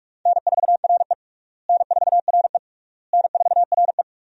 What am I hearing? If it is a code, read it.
Morse code, heard as a keyed tone, about 47 wpm, 700 Hz. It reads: N4RE N4RE N4RE
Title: USA Call Signs - 4 Characters - Repeated 3x - 45wpm